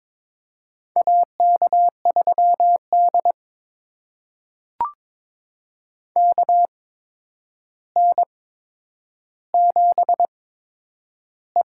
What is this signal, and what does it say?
Morse code, heard as a keyed tone, about 23 wpm, spaced as for 10 wpm, 700 Hz. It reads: AK3D KN7E
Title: USA Call Signs - 4 Characters - 8x Character Spacing - 22wpm